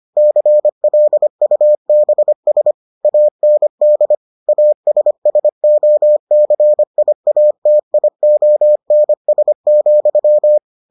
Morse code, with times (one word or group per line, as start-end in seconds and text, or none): 0.2-2.7 CLUBS
3.0-4.2 AND
4.5-10.6 ASSOCIATIONS,